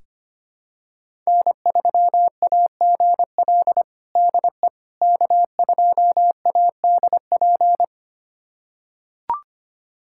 Morse code, heard as a keyed tone, about 25 wpm, 700 Hz. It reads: N3AGL DE K2ADP
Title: USA Callsign DE Callsign - 5 Characters - 25wpm